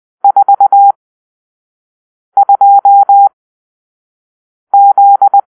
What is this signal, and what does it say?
Morse code, heard as a keyed tone, about 20 wpm, spaced as for 5 wpm, 800 Hz. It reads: 42Z
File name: arrl / Grp-6 Drill-3